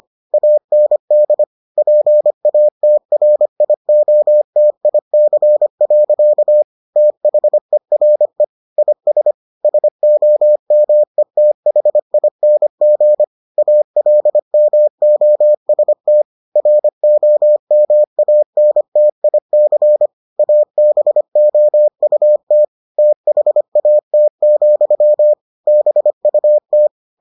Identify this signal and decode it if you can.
Morse code, heard as a keyed tone, about 25 wpm, 600 Hz. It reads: AND PATRIOTIC. THERE IS SOMETHING ALMOST ROMANTIC ABOUT THAT, BUT